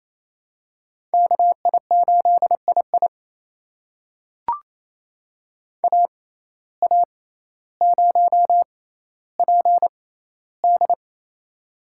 Morse code, heard as a keyed tone, about 28 wpm, spaced as for 14 wpm, 700 Hz. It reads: KI8II AA0PD
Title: USA Call Signs - 5 Characters - 6x Character Spacing - 28wpm